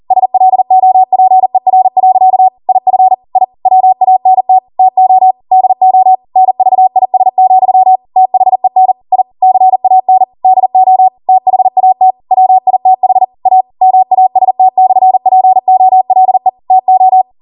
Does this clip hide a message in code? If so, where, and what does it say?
Yes: Morse code from 0.1 to 17.2 s, heard as a keyed tone, about 40 wpm, 750 Hz. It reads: SLOPER. IF I WANT TO DO NVIS, THEN I CAN DO THAT WITH A MAST/POLE TO